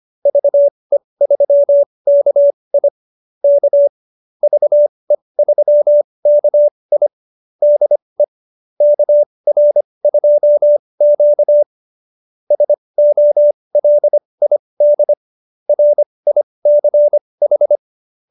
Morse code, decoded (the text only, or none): VE3KI K VE3KI DE KR2Q SOLID RICH